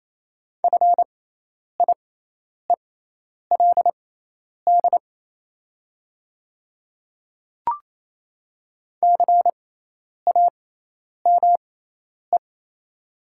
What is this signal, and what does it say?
Morse code, heard as a keyed tone, about 28 wpm, spaced as for 10 wpm, 700 Hz. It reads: FIELD CAME
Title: 6x Character Spacing - Top 500 Words - 28wpm